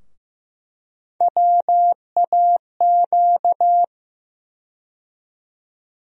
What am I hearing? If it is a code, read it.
Morse code, heard as a keyed tone, about 15 wpm, 700 Hz. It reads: WAQ
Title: USA Call Sign Postfixes - 3 Letters 15wpm